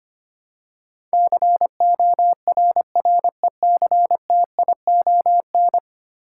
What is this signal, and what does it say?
Morse code, heard as a keyed tone, about 25 wpm, 700 Hz. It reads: CORRECTION
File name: 10 Letter Words - First 3 Letters Given - 25wpm